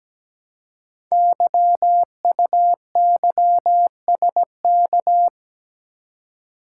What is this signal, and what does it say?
Morse code, heard as a keyed tone, about 17 wpm, 700 Hz. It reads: YUYSK
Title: Code Groups - 5 Letters - 17wpm